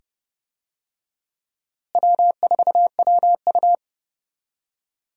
Morse code, heard as a keyed tone, about 30 wpm, 700 Hz. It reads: W4WU
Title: USA Call Signs - 4 Characters - Repeated 2x - 30wpm